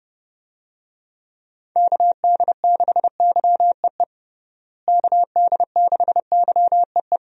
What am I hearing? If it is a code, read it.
Morse code, heard as a keyed tone, about 30 wpm, 700 Hz. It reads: KD6YEE KD6YEE
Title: USA Call Signs - 6 Characters - Repeated 3x - 30wpm